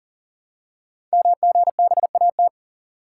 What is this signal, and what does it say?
Morse code, heard as a keyed tone, about 41 wpm, 690 Hz. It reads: MGBAT
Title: Code Groups - 5 Letter:Numbers - 40wpm